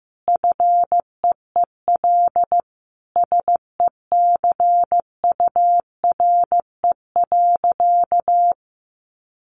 Morse code, heard as a keyed tone, about 15 wpm, 700 Hz. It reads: FEEL SECURE.
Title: Sentences from Top 1500 Words 15wpm